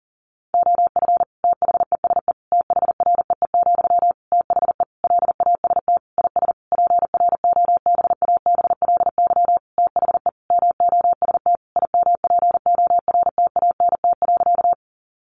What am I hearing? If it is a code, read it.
Morse code, heard as a keyed tone, about 37 wpm, 700 Hz. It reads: OF THESE THREE, THE LAST IS PROBABLY THE MOST IMPORTANT.
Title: Sentences from Top 700 Words 40wpm